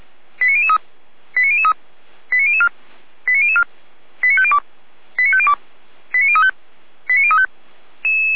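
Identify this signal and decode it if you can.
Five-tone selective call (ZVEI), heard as quick runs of five tones, tones of about 80 ms. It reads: 89003 89003 89004 89004 89502 89502 89035 89035
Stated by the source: ZVEI five-tone call 89001 received by SDR, recorded by feuerwehr-orlamuende.org fire brigade.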